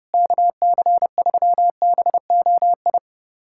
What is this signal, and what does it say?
Morse code, heard as a keyed tone, about 32 wpm, 700 Hz. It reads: KC3BOI
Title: USA Callsign DE Callsign 30wpm